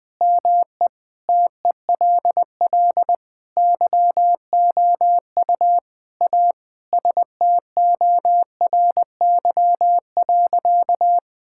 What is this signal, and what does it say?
Morse code, heard as a keyed tone, about 20 wpm, 700 Hz. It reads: ME TELL YOU A STORY.
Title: Sentences with 4-Word Phrases - 20wpm